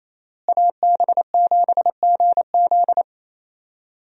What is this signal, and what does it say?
Morse code, heard as a keyed tone, about 28 wpm, 700 Hz. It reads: AB7GZ
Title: USA Call Signs - 5 Characters - Repeated 2x - 28wpm